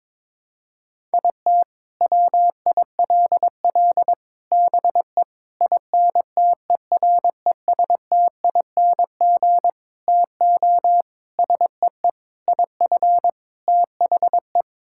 Morse code, heard as a keyed tone, about 21 wpm, 700 Hz. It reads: IT WILL BE INTERESTING TO SEE IF THE
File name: practice / 22wpm / Sets of 8 Words - 22wpm